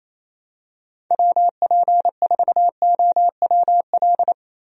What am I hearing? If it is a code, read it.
Morse code, heard as a keyed tone, about 28 wpm, 700 Hz. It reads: WP4OWL
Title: USA Call Signs - 6 Characters - Repeated 3x - 28wpm